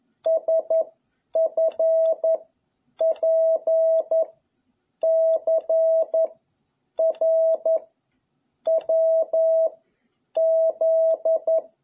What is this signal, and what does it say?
Morse code, heard as a keyed tone, about 11 wpm, 650 Hz. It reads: S F P C R W Z
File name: jh8cbh / 2021cw